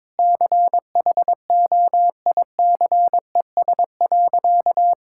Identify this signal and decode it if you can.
Morse code, heard as a keyed tone, about 21 wpm, 700 Hz. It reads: CHOICES.
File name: practice / 22wpm / Sentences with 3-Word Phrases - 22wpm